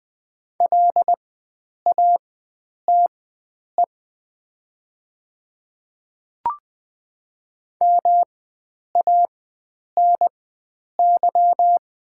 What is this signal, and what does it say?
Morse code, heard as a keyed tone, about 20 wpm, spaced as for 9 wpm, 700 Hz. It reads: LATE MANY
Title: 4x Character Spacing - Top 300 Words - 20wpm